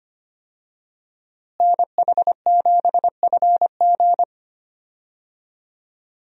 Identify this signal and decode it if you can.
Morse code, heard as a keyed tone, about 25 wpm, 700 Hz. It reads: NH7FG
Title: USA Call Signs - 5 Characters - Repeated 2x - 25wpm